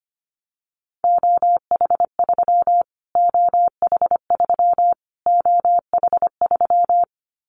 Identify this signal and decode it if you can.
Morse code, heard as a keyed tone, about 25 wpm, 700 Hz. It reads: OH3 OH3 OH3
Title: Callsign Prefixes for Top 25 DXCC - Repeated 3x - 25wpm